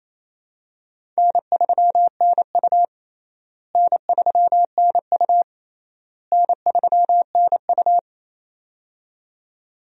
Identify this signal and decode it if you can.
Morse code, heard as a keyed tone, about 28 wpm, 700 Hz. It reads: N3NU N3NU N3NU
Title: USA Call Signs - 4 Characters - Repeated 3x - 28wpm